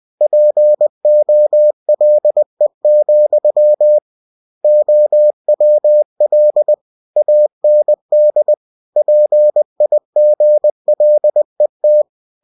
Morse code, heard as a keyed tone, about 20 wpm, 600 Hz. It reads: POLE, OWL AND PIGLET
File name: cw-books / winnie-the-pooh / winnie-20wpm-page-0081